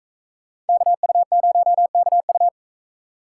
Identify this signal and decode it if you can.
Morse code, heard as a keyed tone, about 40 wpm, 700 Hz. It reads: KU0KU